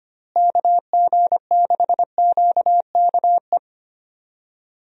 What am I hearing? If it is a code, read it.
Morse code, heard as a keyed tone, about 25 wpm, 700 Hz. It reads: KG6QKE
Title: USA Call Signs - 6 Characters - Repeated 2x - 25wpm